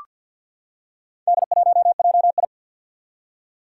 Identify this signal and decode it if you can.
Morse code, heard as a keyed tone, about 47 wpm, 700 Hz. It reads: D1JI
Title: Four Challenging Letter-Number Combinations 50wpm